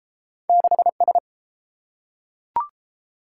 Morse code, heard as a keyed tone, about 30 wpm, 700 Hz. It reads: 6S